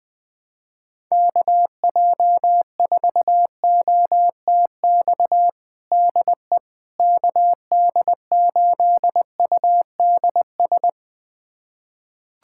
Morse code, heard as a keyed tone, about 20 wpm, 700 Hz. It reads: KJ4OTX DE KD8UDS
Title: USA Callsign DE Callsign - 6 Characters - 20wpm